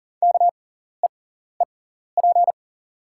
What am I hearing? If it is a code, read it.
Morse code, heard as a keyed tone, about 40 wpm, spaced as for 14 wpm, 690 Hz. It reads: KEEP
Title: 6x Character Spacing - Top 300 Words - 40wpm